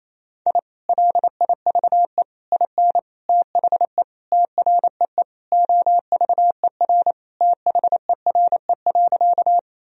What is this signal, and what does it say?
Morse code, heard as a keyed tone, about 28 wpm, 700 Hz. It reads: I LIVE IN THE TREE OVER THERE.
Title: Sentences from Top 1000 Words - Review Part 1 - 28wpm